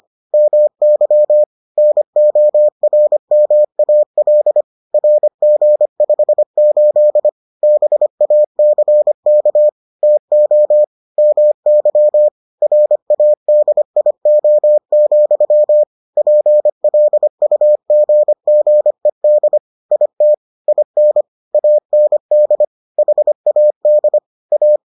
Morse code, heard as a keyed tone, about 25 wpm, 600 Hz. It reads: MY NORMAL RG58 BACK TO MY RADIO, PLUGGED IT IN AND HAD A